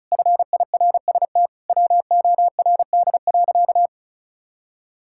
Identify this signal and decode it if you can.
Morse code, heard as a keyed tone, about 36 wpm, 700 Hz. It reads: FIRST WORD.
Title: Pangram Challenge v2 - 35wpm